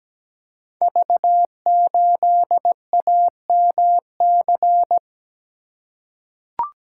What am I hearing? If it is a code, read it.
Morse code, heard as a keyed tone, about 17 wpm, 700 Hz. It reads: V8AMC
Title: Code Groups - 5 Letter:Numbers - 17wpm